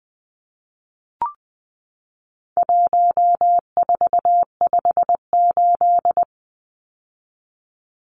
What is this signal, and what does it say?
Morse code, heard as a keyed tone, about 20 wpm, 700 Hz. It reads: 1458
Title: Numbers - 4-Digit 20wpm